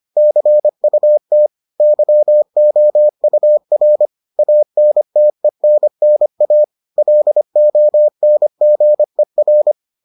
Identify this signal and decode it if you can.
Morse code, heard as a keyed tone, about 25 wpm, 600 Hz. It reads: CUT YOUR ANTENNA LONGER